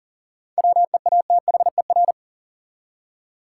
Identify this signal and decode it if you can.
Morse code, heard as a keyed tone, about 37 wpm, 700 Hz. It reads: WEATHER
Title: Seven Letter Words - 40wpm